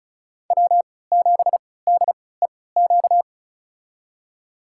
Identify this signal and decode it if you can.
Morse code, heard as a keyed tone, about 36 wpm, spaced as for 20 wpm, 690 Hz. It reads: W7DEQ
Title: USA Call Signs - 3x Character Spacing - 35wpm